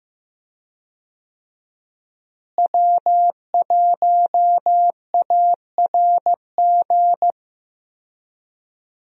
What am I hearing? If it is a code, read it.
Morse code, heard as a keyed tone, about 15 wpm, 700 Hz. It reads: W1ARG
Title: USA Call Signs - 5 Characters - Repeated 2x - 15wpm